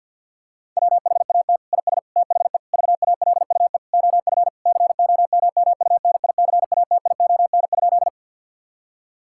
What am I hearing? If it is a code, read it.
Morse code, heard as a keyed tone, about 46 wpm, 700 Hz. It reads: WHAT IS THE VALUE OF COMMUNICATION?
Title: Sentences from Top 1000 Words - Review Part 6 - 50wpm